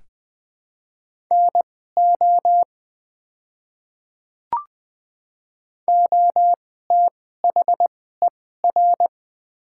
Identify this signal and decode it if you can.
Morse code, heard as a keyed tone, about 20 wpm, spaced as for 15 wpm, 700 Hz. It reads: NO OTHER
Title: 2x Character Spacing - Top 100 Words - 20wpm